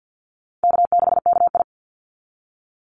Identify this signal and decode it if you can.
Morse code, heard as a keyed tone, about 50 wpm, 700 Hz. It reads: K6KI
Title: Call Signs - 50wpm